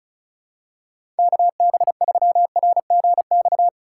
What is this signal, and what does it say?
Morse code, heard as a keyed tone, about 36 wpm, 700 Hz. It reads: KB3RGX
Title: USA Call Signs - 6 Characters - 4x Character Spacing - 35wpm